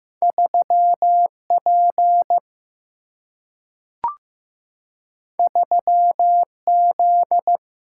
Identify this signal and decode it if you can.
Morse code, heard as a keyed tone, about 15 wpm, 700 Hz. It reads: 3 P 3 Z
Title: Single-Letter Single-Number 15wpm